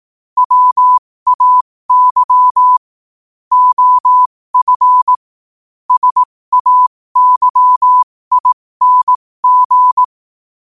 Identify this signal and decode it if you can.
Morse code, heard as a keyed tone, about 18 wpm, spaced as for 15 wpm, 1000 Hz. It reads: WAY OF SAYING